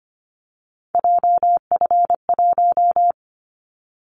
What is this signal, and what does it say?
Morse code, heard as a keyed tone, about 25 wpm, 700 Hz. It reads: JF1
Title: Callsign Prefixes for Top 25 DXCC - 25wpm